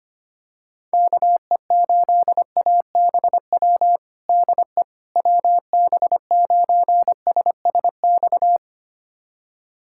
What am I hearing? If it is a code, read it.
Morse code, heard as a keyed tone, about 25 wpm, 700 Hz. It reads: KE8ABW DE WB9SSX
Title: USA Callsign DE Callsign - 6 Characters - 25wpm